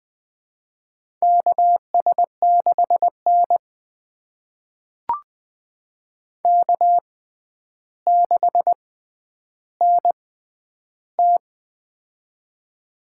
Morse code, heard as a keyed tone, about 20 wpm, spaced as for 10 wpm, 700 Hz. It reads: KS6N K6NT
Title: USA Call Signs - 4 Characters - 6x Character Spacing - 20wpm